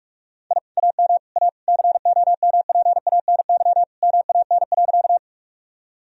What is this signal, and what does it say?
Morse code, heard as a keyed tone, about 41 wpm, 700 Hz. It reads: I AM A COMPANY MAN.
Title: Sentences from Top 1000 Words - Review Part 3 - 45wpm